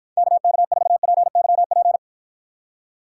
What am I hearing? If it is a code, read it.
Morse code, heard as a keyed tone, about 53 wpm, 700 Hz. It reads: KK4PYP